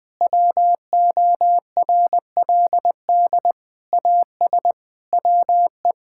Morse code, heard as a keyed tone, about 20 wpm, 700 Hz. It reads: WORLD AS WE
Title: Sentences with 8-Word Phrases - 20wpm